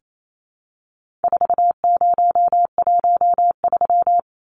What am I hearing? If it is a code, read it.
Morse code, heard as a keyed tone, about 28 wpm, 700 Hz. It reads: 4013
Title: Numbers - 4-Digit 28wpm